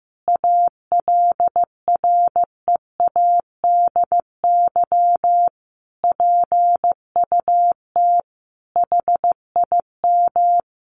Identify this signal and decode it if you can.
Morse code, heard as a keyed tone, about 14 wpm, 700 Hz. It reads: ALREADY PUT HIM
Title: Sentences from Top 1200 Words 15wpm